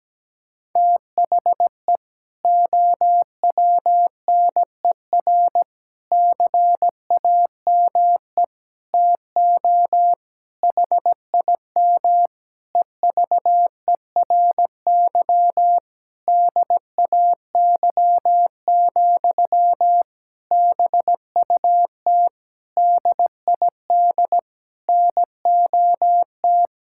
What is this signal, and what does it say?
Morse code, heard as a keyed tone, about 17 wpm, 700 Hz. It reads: THE OWNER CAME TO HIM EVERY DAY, BUT DID NOT
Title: Sentences from Top 1600 Words 17wpm